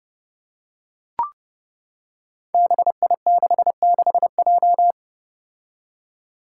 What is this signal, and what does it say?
Morse code, heard as a keyed tone, about 30 wpm, 700 Hz. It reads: BI66J
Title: Five Challenging Letter-Number Combinations 30wpm